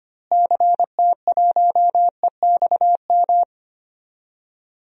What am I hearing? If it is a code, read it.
Morse code, heard as a keyed tone, about 25 wpm, 700 Hz. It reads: CT1EXM
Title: DXCC-100 Call Signs - 6 Characters - Repeated 2x - 25wpm